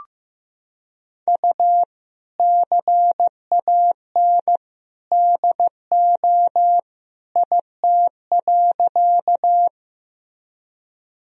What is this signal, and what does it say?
Morse code, heard as a keyed tone, about 15 wpm, 700 Hz. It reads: U CAN DO IT.
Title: Sentences from Top 100 Words and Abbreviations 15wpm